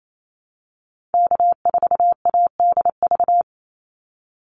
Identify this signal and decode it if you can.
Morse code, heard as a keyed tone, about 28 wpm, 700 Hz. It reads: K4ADV
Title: NA SOTA Activator Callsigns 28wpm